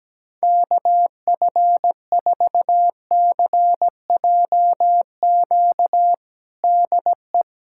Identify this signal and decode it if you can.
Morse code, heard as a keyed tone, about 17 wpm, 700 Hz. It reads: KF4CJQ DE
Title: USA Callsign DE Callsign - 6 Characters - 17wpm